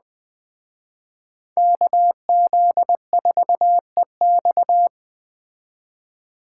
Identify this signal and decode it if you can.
Morse code, heard as a keyed tone, about 20 wpm, 700 Hz. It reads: KZ4EX